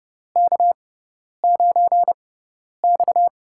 Morse code, heard as a keyed tone, about 30 wpm, spaced as for 10 wpm, 700 Hz. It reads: K9X
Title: USA Call Signs - 4 Characters - 6x Character Spacing - 30wpm